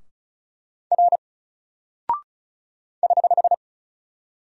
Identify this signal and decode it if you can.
Morse code, heard as a keyed tone, about 36 wpm, 700 Hz. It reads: R <HH>
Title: Single Character (Letter:Number:Punctuation:Prosign) 35wpm